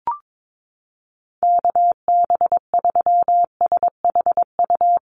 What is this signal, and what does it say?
Morse code, heard as a keyed tone, about 22 wpm, 700 Hz. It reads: KB3SHU
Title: USA Call Signs Repeated 3x 22wpm